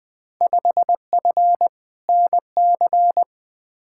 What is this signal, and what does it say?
Morse code, heard as a keyed tone, about 20 wpm, 700 Hz. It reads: 5F NC
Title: Field Day Exchanges Encore - 20wpm